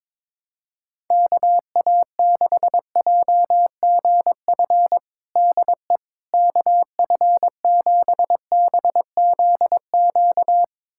Morse code, heard as a keyed tone, about 22 wpm, 700 Hz. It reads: KA6JGF DE KF7BZQ
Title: USA Callsign DE Callsign - 6 Characters - 22wpm